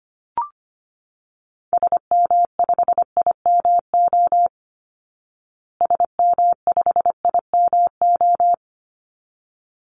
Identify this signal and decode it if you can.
Morse code, heard as a keyed tone, about 25 wpm, 700 Hz. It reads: SM5IMO SM5IMO
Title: Call Signs Repeated 3x - 25wpm